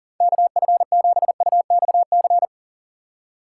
Morse code, heard as a keyed tone, about 37 wpm, 700 Hz. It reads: KF7UXC
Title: K1USN SST QSO Practice - 40wpm